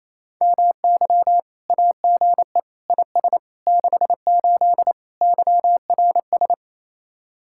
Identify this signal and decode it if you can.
Morse code, heard as a keyed tone, about 34 wpm, 700 Hz. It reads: MY AGE IS 68 YRS
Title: Speed-Racing Common QSO Phrases - 28wpm